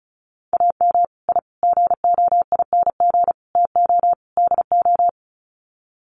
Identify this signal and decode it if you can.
Morse code, heard as a keyed tone, about 34 wpm, 700 Hz. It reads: AM I GOING TO DO
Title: Sets of 5 Words Encore 35wpm